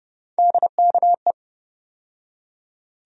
Morse code, heard as a keyed tone, about 30 wpm, 700 Hz. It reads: DKE
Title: USA Call Sign Postfixes - 3 Letters 30wpm